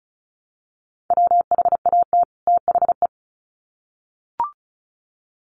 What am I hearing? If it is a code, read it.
Morse code, heard as a keyed tone, about 34 wpm, 700 Hz. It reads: WHAT THE
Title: Sets of 2 Words 35wpm